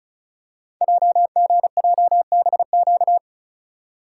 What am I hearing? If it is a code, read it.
Morse code, heard as a keyed tone, about 36 wpm, 700 Hz. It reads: JGJBQ
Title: Code Groups - 5 Letters - 35wpm